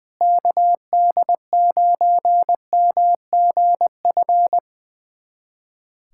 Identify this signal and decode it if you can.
Morse code, heard as a keyed tone, about 20 wpm, 700 Hz. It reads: KD9MGF
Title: USA Callsign DE Callsign - 6 Characters - 20wpm